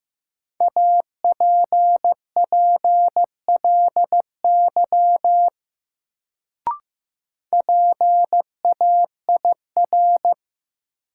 Morse code, heard as a keyed tone, about 15 wpm, 700 Hz. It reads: APPLY PAIR
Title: Sentences from Top 1200 Words - New Word Drill Alt - 15wpm